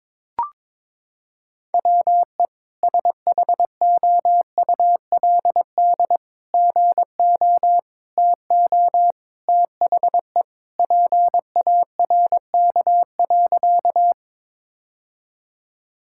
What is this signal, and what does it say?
Morse code, heard as a keyed tone, about 22 wpm, 700 Hz. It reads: WE SHOULD GO TO THE PARK.
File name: practice / 22wpm / Sentences with 3-Word Phrases - 22wpm